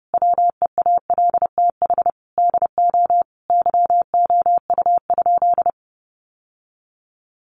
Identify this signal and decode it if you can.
Morse code, heard as a keyed tone, about 30 wpm, 700 Hz. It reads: WEALTH DO YOU?
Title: Sentences from Top 1200 Words 30wpm